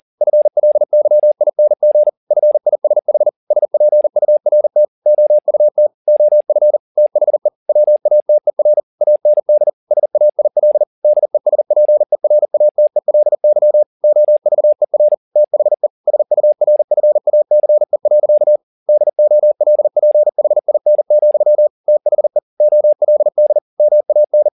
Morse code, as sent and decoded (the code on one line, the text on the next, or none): ..-. .-.. -.-- .. -. --. / ..-. .. ... .... / ... .--. ..- .-. - / --- ..- - / --- ..-. / - .... / .-- .- - . .-. / .- -. -.. / ... .- .. .-.. / -.. ... .--. .-. .- - . .-.. -.-- / --- ...- . .-. / - .... / ... ..- .-. ..-. .- -.-. .-.-.- / -.. --- .-.. .--. .... .. -. --..-- / - .... / --- .-.. -.. / -- .- -.
FLYING FISH SPURT OUT OF THE WATER AND SAIL DESPERATELY OVER THE SURFACE. DOLPHIN, THE OLD MAN